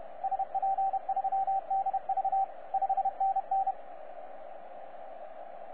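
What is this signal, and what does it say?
Morse code, heard as a keyed tone, about 31 wpm, 780 Hz, with radio noise. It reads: SP3DV 5NN
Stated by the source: 9U4U on 15 metre CW, DX station recording by DX World.net.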